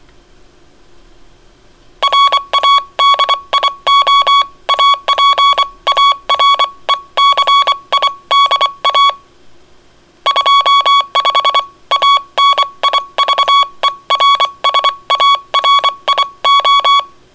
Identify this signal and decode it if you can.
Morse code, heard as a keyed tone, about 24 wpm, spaced as for 21 wpm, 1120 Hz. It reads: RADIO APARECIDA 25 ANIVERSARIO